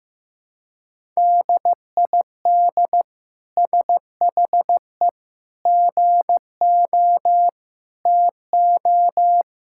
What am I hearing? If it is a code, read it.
Morse code, heard as a keyed tone, about 15 wpm, 700 Hz. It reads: DID SHE GO TO
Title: Sentences from Top 1000 Words - Review Part 6 - 15wpm